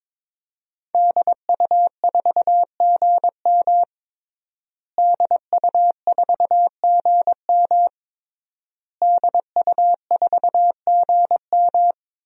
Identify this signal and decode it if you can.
Morse code, heard as a keyed tone, about 22 wpm, 700 Hz. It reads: DU4GM DU4GM DU4GM
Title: DXCC-100 Call Signs - 5 Characters - Repeated 3x - 22wpm